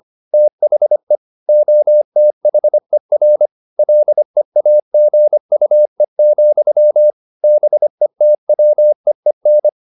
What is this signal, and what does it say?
Morse code, heard as a keyed tone, about 25 wpm, 600 Hz. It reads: THE OTHER LEAGUE, BETWEEN
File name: the old man and the sea part 1 25wpm